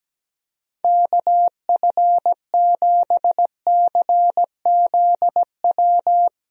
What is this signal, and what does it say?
Morse code, heard as a keyed tone, about 17 wpm, 700 Hz. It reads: KF7CZW
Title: USA Call Signs - 6 Characters - Repeated 3x - 17wpm